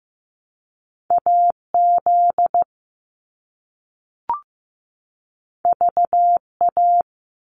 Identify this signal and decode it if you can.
Morse code, heard as a keyed tone, about 15 wpm, 700 Hz. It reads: AZ VA
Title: US States 15wpm